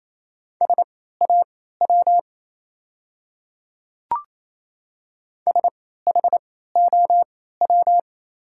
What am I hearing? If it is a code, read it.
Morse code, heard as a keyed tone, about 28 wpm, spaced as for 16 wpm, 700 Hz. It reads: SAW SHOW
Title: 3x Character Spacing - Top 300 Words - 28wpm